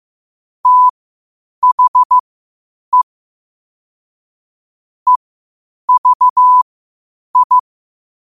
Morse code, heard as a keyed tone, about 15 wpm, spaced as for 8 wpm, 1000 Hz. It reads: THE EVI
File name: qotdfiles / 8wpm